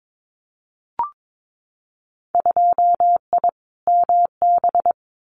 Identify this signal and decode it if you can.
Morse code, heard as a keyed tone, about 22 wpm, 700 Hz. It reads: 2I MB
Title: Winter Field Day Exchanges Encore 22wpm